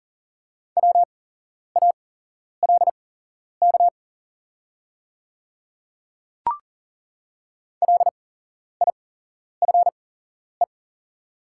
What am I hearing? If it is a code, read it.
Morse code, heard as a keyed tone, about 40 wpm, spaced as for 11 wpm, 700 Hz. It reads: WALK LIFE
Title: 8x Character Spacing - Top 300 Words - 40wpm